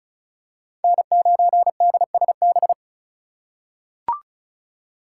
Morse code, heard as a keyed tone, about 34 wpm, 690 Hz. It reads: N9DSB
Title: USA Call Signs - 4x Character Spacing - 35wpm